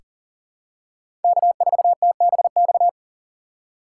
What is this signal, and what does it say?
Morse code, heard as a keyed tone, about 40 wpm, 700 Hz. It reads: K4TBX
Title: USA Call Signs - 5 Characters - 3x Character Spacing - 40wpm